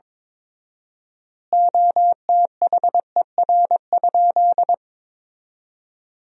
Morse code, heard as a keyed tone, about 22 wpm, 700 Hz. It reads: OTHER?
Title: 4x Word-Spacing - Sentences from Top 200 Words - 22wpm